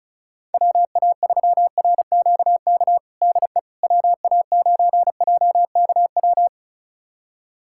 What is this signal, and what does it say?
Morse code, heard as a keyed tone, about 36 wpm, 700 Hz. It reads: WA3RQK DE WA9JKW
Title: USA Callsign DE Callsign - 6 Characters - 35wpm